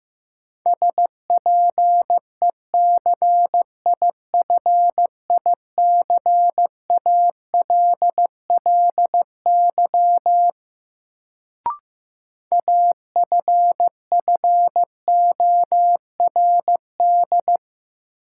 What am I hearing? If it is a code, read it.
Morse code, heard as a keyed tone, about 15 wpm, 700 Hz. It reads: SPECIFICALLY AFFORD
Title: Sentences from Top 1100 Words - New Word Drill - 15wpm